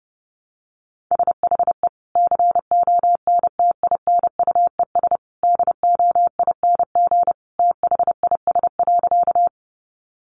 Morse code, heard as a keyed tone, about 30 wpm, 700 Hz. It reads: SHE CONTINUES DOING THIS.
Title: Easy Phrases 30wpm